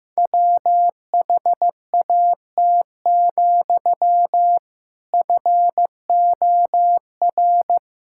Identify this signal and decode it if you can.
Morse code, heard as a keyed tone, about 15 wpm, 700 Hz. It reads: WHAT, FOR
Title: Sentences from Top 1000 Words - Review Part 5 - 15wpm